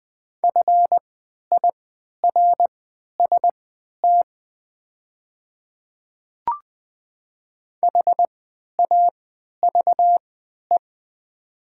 Morse code, heard as a keyed tone, about 20 wpm, spaced as for 11 wpm, 700 Hz. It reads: FIRST HAVE
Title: 3x Character Spacing - Top 200 Words - 20wpm